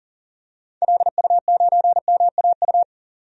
Morse code, heard as a keyed tone, about 38 wpm, 700 Hz. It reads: LU9MAU
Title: DXCC-100 Call Signs - 6 Characters - 40wpm